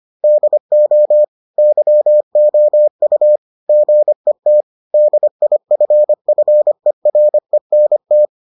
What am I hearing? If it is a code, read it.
Morse code, heard as a keyed tone, about 25 wpm, 600 Hz. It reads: DO YOU GET DIFFERENT